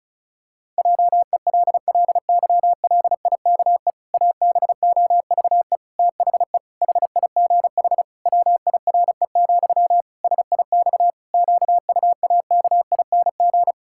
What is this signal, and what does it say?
Morse code, heard as a keyed tone, about 36 wpm, 700 Hz. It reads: JELLYLIKE ABOVE THE HIGH WIRE, SIX QUAKING
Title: Pangram Challenge v2 - 35wpm